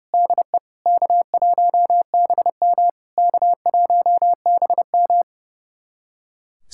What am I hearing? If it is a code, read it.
Morse code, heard as a keyed tone, about 30 wpm, 700 Hz. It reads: DE K1BM K1BM
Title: SOTA QSO Practice 30wpm